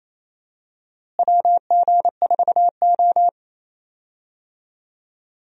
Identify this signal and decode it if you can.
Morse code, heard as a keyed tone, about 28 wpm, 700 Hz. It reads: WG4O
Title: USA Call Signs - 4 Characters - Repeated 2x - 28wpm